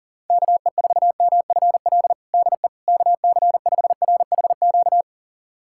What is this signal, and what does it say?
Morse code, heard as a keyed tone, about 40 wpm, 700 Hz. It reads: KE4MFL DE KC5RHQ
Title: USA Callsign DE Callsign 40wpm